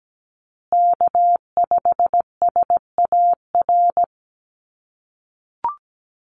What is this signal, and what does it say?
Morse code, heard as a keyed tone, about 17 wpm, 700 Hz. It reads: K5SAR
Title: USA Call Signs Repeated 3x 17wpm